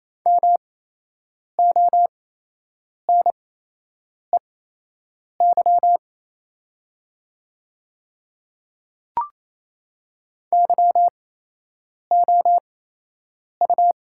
Morse code, heard as a keyed tone, about 28 wpm, spaced as for 8 wpm, 700 Hz. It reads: MONEY YOU
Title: 8x Character Spacing - Top 500 Words - 28wpm